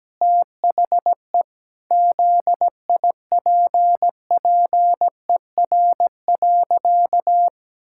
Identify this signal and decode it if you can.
Morse code, heard as a keyed tone, about 17 wpm, 700 Hz. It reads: THE ZIPPER.
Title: Pangram Challenge v2 - 17wpm